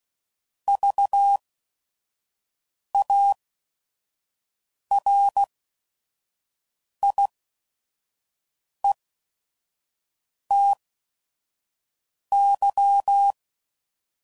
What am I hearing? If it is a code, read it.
Morse code, heard as a keyed tone, about 16 wpm, spaced as for 5 wpm, 800 Hz. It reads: VARIETY
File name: Prefixes / By First Letter / Vs